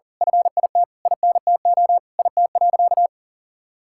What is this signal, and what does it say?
Morse code, heard as a keyed tone, about 40 wpm, 700 Hz. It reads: FIT INTO IT.
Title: Sentences from Top 1000 Words - Review Part 5 - 40wpm